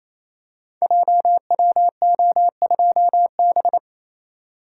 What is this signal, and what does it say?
Morse code, heard as a keyed tone, about 27 wpm, 700 Hz. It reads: JWO2B
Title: Code Groups - 5 Letter:Numbers - 28wpm